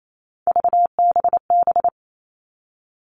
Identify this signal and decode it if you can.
Morse code, heard as a keyed tone, about 28 wpm, 700 Hz. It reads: VBB
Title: Three Challenging Letter-Number Combinations 28wpm